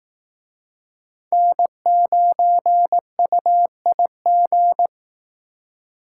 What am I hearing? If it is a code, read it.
Morse code, heard as a keyed tone, about 18 wpm, 700 Hz. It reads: N9UIG
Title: Speed Racing - USA 1x3 Call Signs 15wpm